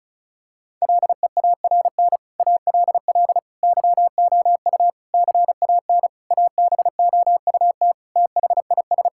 Morse code, heard as a keyed tone, about 34 wpm, 700 Hz. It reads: LEARN ALL YOU CAN ABOUT THIS
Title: Sentences from Top 1000 Words - Review Part 5 - 35wpm